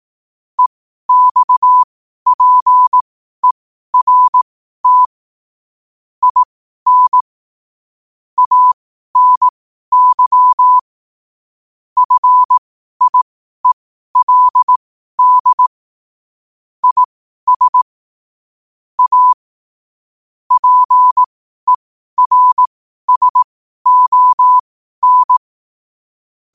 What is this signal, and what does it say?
Morse code, heard as a keyed tone, about 18 wpm, spaced as for 12 wpm, 1000 Hz. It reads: EXPERT IN ANY FIELD IS A PERSON
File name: qotdfiles / 12wpm